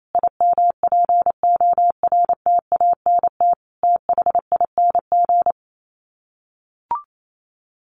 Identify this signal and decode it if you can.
Morse code, heard as a keyed tone, about 28 wpm, 700 Hz. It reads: IMPORTANT THING
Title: Sets of 4 Words 28wpm